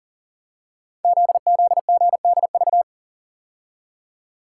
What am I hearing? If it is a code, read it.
Morse code, heard as a keyed tone, about 38 wpm, 690 Hz. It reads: ZZGDV
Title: Code Groups - 5 Letters - 40wpm